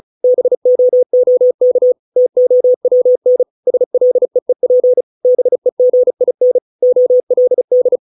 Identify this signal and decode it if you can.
Morse code, heard as a keyed tone, about 34 wpm, 500 Hz. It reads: BOOK TOWN SLEEP BEGIN OLD